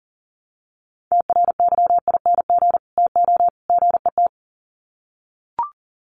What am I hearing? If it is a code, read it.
Morse code, heard as a keyed tone, about 40 wpm, 700 Hz. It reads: TRYING TO GET
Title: Sets of 3 Words 40wpm